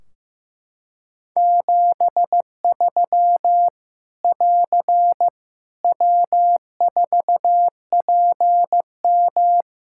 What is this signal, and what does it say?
Morse code, heard as a keyed tone, about 15 wpm, 700 Hz. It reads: 73 <AR> W4PM
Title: K1USN SST QSO Practice 15wpm